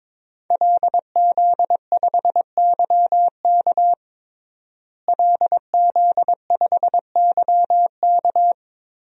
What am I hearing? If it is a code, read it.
Morse code, heard as a keyed tone, about 22 wpm, 700 Hz. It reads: LZ5YK LZ5YK
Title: DXCC-100 Call Signs - 5 Characters - Repeated 2x - 22wpm